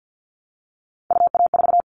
Morse code, heard as a keyed tone, about 50 wpm, 700 Hz. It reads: UA4
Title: Callsign Prefixes for Top 50 DXCC - 50wpm